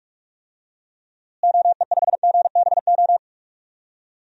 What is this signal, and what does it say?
Morse code, heard as a keyed tone, about 44 wpm, 700 Hz. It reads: OE5GBO
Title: DXCC-100 Call Signs - 6 Characters - Repeated 2x - 45wpm